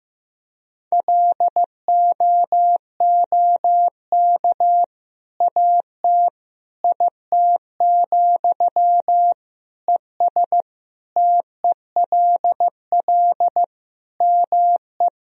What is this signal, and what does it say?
Morse code, heard as a keyed tone, about 15 wpm, 700 Hz. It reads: LOOK AT IT, ES TELL ME
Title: Sentences from Top 200 Words and Abbreviations 15wpm